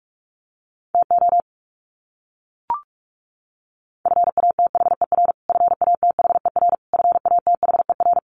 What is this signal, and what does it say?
Morse code, heard as a keyed tone, about 44 wpm, 700 Hz. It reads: TO FATHER FATHER FATHER
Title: Top 300 Words Repeated 45wpm